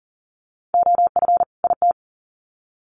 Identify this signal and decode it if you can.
Morse code, heard as a keyed tone, about 40 wpm, 700 Hz. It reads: OF IT
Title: Sets of 2 Words 40wpm